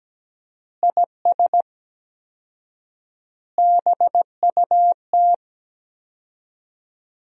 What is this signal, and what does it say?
Morse code, heard as a keyed tone, about 17 wpm, 700 Hz. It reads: IS BUT
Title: 4x Word-Spacing - Sentences from Top 100 Words - 17wpm